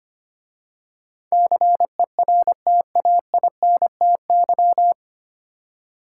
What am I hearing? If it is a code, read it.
Morse code, heard as a keyed tone, about 25 wpm, 700 Hz. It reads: CERTAINTY
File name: practice / 25wpm / 9 Letter Words - First 3 Letters Given - 25wpm